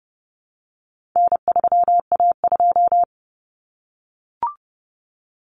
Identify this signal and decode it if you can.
Morse code, heard as a keyed tone, about 30 wpm, 700 Hz. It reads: N3A2
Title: Common Cut Numbers - 4-Digit 30wpm